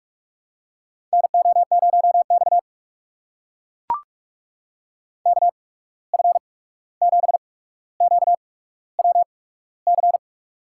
Morse code, heard as a keyed tone, about 42 wpm, spaced as for 17 wpm, 700 Hz. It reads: NO0X KF7QWC
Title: USA Call Signs - 8x Character Spacing - 45wpm